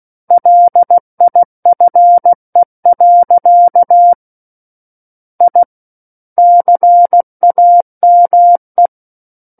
Morse code, heard as a keyed tone, about 16 wpm, 700 Hz. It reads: LIFE. I CAME
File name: Book One / 07-Chapter 7 - 16 WPM